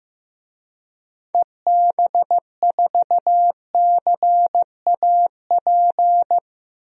Morse code, heard as a keyed tone, about 15 wpm, 700 Hz. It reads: EB4CAP
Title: DXCC-100 Call Signs - 6 Characters - 15wpm